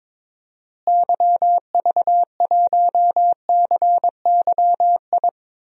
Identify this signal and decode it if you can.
Morse code, heard as a keyed tone, about 21 wpm, 700 Hz. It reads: YV1CYI
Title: DXCC-100 Call Signs - 6 Characters - Repeated 2x - 22wpm